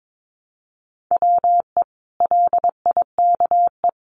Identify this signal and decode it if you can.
Morse code, heard as a keyed tone, about 22 wpm, 700 Hz. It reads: WE LIKE
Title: Sentences from Top 1100 Words 22wpm